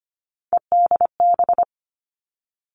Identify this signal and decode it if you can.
Morse code, heard as a keyed tone, about 25 wpm, 700 Hz. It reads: EDB
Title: Three Challenging Letter-Number Combinations 25wpm